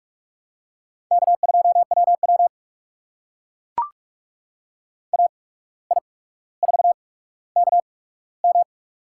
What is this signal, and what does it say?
Morse code, heard as a keyed tone, about 44 wpm, spaced as for 18 wpm, 700 Hz. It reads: K2WW AI4KM
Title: USA Call Signs - 8x Character Spacing - 45wpm